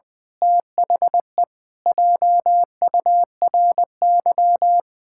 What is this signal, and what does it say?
Morse code, heard as a keyed tone, about 20 wpm, 700 Hz. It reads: THE JURY
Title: Pangram Challenge v2 - 20wpm